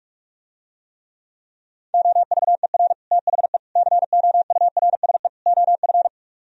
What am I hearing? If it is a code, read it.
Morse code, heard as a keyed tone, about 50 wpm, 700 Hz. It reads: OVER THE COURSE OF